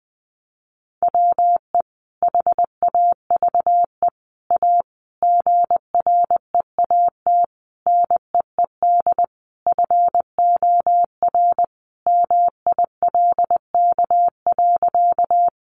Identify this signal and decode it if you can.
Morse code, heard as a keyed tone, about 20 wpm, 700 Hz. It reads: WE HAVE A GREAT NEED FOR MILK.
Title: Sentences from Top 1500 Words 20wpm